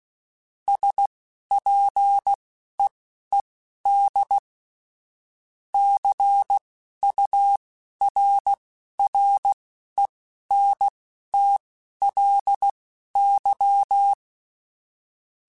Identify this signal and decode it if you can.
Morse code, heard as a keyed tone, about 16 wpm, spaced as for 11 wpm, 800 Hz. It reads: SPEED CURRENTLY
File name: Weather Sets / Weather Sets-mp3s / Weather 5 5 @10